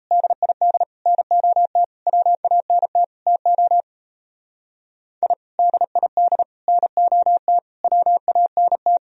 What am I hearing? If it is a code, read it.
Morse code, heard as a keyed tone, about 34 wpm, 700 Hz. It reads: DID NOT WANT TO I DID NOT WANT
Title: Speed-Racing - Sets of 5 Words 28wpm